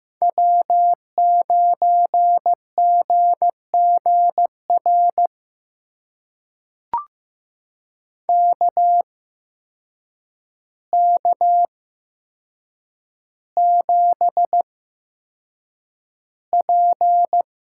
Morse code, heard as a keyed tone, about 15 wpm, spaced as for 7 wpm, 700 Hz. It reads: W9GGR KK7P
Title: USA Call Signs - 5 Characters - 8x Character Spacing - 15wpm